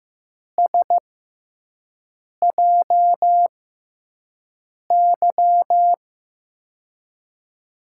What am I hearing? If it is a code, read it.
Morse code, heard as a keyed tone, about 15 wpm, spaced as for 5 wpm, 700 Hz. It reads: SJY
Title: USA Call Signs - 6 Characters - 6x Character Spacing - 15wpm